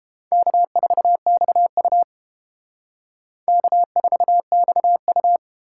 Morse code, heard as a keyed tone, about 30 wpm, 700 Hz. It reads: K4XU K4XU